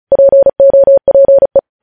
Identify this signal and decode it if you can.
Morse code, heard as a keyed tone, about 36 wpm, 550 Hz. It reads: POPE
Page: News Headlines in Morse Code at 35 WPM